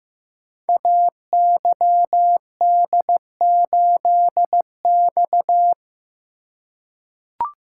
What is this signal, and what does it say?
Morse code, heard as a keyed tone, about 15 wpm, 700 Hz. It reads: AYD8X
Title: Code Groups - 5 Letter:Numbers - 15wpm